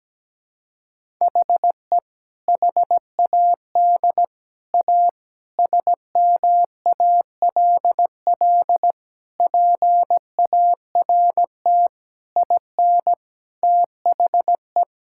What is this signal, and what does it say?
Morse code, heard as a keyed tone, about 17 wpm, 700 Hz. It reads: HE HAD A SMALL PART IN THE
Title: Sentences from Top 1000 Words - Review Part 5 - 17wpm